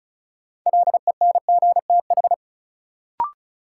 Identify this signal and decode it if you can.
Morse code, heard as a keyed tone, about 36 wpm, 700 Hz. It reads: LENGTH
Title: Sentences from Top 1000 Words - New Word Drill - 35wpm